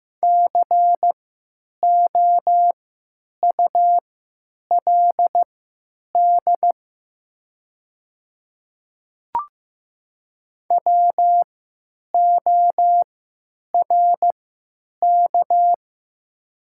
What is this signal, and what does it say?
Morse code, heard as a keyed tone, about 15 wpm, spaced as for 9 wpm, 700 Hz. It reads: COULD WORK
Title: 3x Character Spacing - Top 300 Words - 15wpm